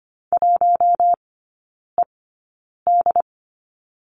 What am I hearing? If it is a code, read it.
Morse code, heard as a keyed tone, about 25 wpm, spaced as for 9 wpm, 700 Hz. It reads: 1ED
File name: course / 25:10wpm / 093 Lesson 16e - Letter c - Three Characters 25wpm